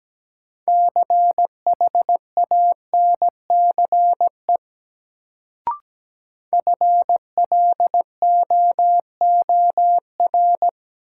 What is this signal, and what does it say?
Morse code, heard as a keyed tone, about 17 wpm, 700 Hz. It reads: CHANCE FLOOR
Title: Sets of 3 Words - New Word Drill - 17wpm